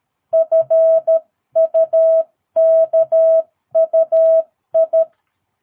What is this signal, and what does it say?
Morse code, heard as a keyed tone, about 13 wpm, 660 Hz. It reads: FUKUI
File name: jh8cbh / 2022cw